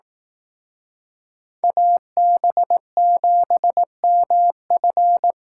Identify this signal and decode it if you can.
Morse code, heard as a keyed tone, about 18 wpm, 700 Hz. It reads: AB7MF
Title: Speed Racing - USA 2x2 Call Signs 15wpm